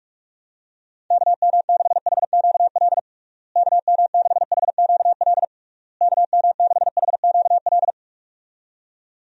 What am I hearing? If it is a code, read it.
Morse code, heard as a keyed tone, about 41 wpm, 700 Hz. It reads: KM6HQL KM6HQL KM6HQL
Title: USA Call Signs - 6 Characters - Repeated 3x - 45wpm